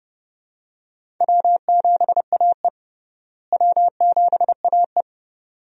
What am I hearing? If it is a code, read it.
Morse code, heard as a keyed tone, about 32 wpm, 700 Hz. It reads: W7AE W7AE
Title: USA Call Signs - 4 Characters - Repeated 2x - 30wpm